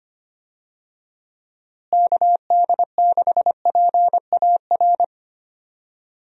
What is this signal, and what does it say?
Morse code, heard as a keyed tone, about 25 wpm, 700 Hz. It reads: KD6PAR
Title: USA Call Signs - 6 Characters - Repeated 3x - 25wpm